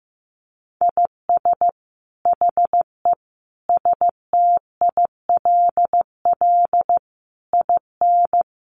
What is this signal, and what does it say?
Morse code, heard as a keyed tone, about 15 wpm, 700 Hz. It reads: IS HE STILL IN
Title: Sentences from Top 800 Words 15wpm